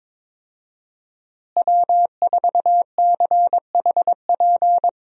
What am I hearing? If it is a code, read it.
Morse code, heard as a keyed tone, about 22 wpm, 700 Hz. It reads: W4CHP
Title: USA Call Signs - 5 Characters - Repeated 2x - 22wpm